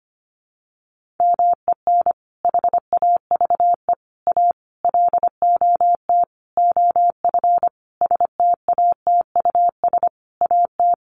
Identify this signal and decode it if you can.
Morse code, heard as a keyed tone, about 25 wpm, 700 Hz. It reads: MEN HAVE A LOT OF STATUS AT
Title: Sentences from Top 1400 Words 25wpm